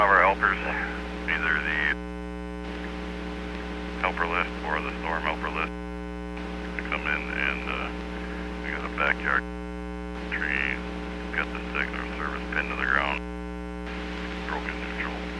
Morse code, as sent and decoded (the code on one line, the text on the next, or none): none